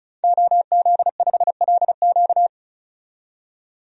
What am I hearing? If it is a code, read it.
Morse code, heard as a keyed tone, about 36 wpm, 700 Hz. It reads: OZ5LQ
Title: DXCC-100 Call Signs - 5 Characters - 35wpm